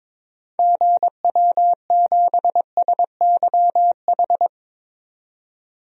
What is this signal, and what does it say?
Morse code, heard as a keyed tone, about 22 wpm, 700 Hz. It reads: GW7SYH
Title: DXCC-100 Call Signs - 6 Characters - 22wpm